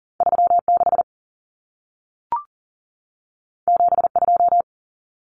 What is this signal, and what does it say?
Morse code, heard as a keyed tone, about 40 wpm, 700 Hz. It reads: 36 72
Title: Common Cut Numbers - 2-Digit 40wpm